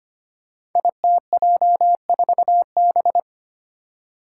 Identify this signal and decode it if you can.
Morse code, heard as a keyed tone, about 25 wpm, 700 Hz. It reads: ITJ4B
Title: Five Challenging Letter-Number Combinations 25wpm